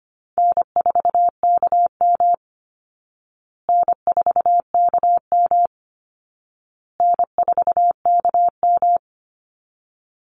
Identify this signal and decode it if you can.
Morse code, heard as a keyed tone, about 25 wpm, 700 Hz. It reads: N4KM N4KM N4KM
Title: Call Signs Repeated 2x - 25wpm